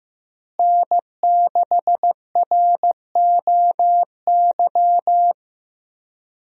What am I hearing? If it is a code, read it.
Morse code, heard as a keyed tone, about 15 wpm, 700 Hz. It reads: N6ROY
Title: USA Call Signs - 5 Characters - Repeated 3x - 15wpm